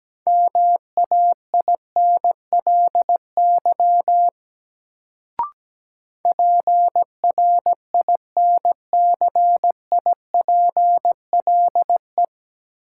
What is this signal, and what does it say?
Morse code, heard as a keyed tone, about 17 wpm, 700 Hz. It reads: MAINLY PRINCIPLE
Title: Sentences from Top 1400 Words - New Word Drill - 17wpm